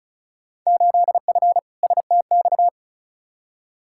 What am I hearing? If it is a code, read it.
Morse code, heard as a keyed tone, about 36 wpm, 700 Hz. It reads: 8F STX
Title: Field Day Exchanges Encore - 35wpm